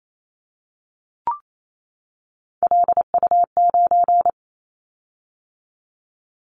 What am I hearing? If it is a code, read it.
Morse code, heard as a keyed tone, about 28 wpm, 700 Hz. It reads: LU9
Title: Callsign Prefixes for Top 100 DXCC - 28wpm